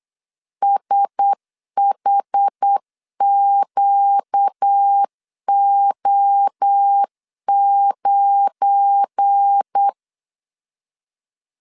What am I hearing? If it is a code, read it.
Morse code, heard as a keyed tone, about 8 wpm, 800 Hz. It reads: SHQO9
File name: LEZIONI / 10 SIGNS / 10B INT 8wpm 3534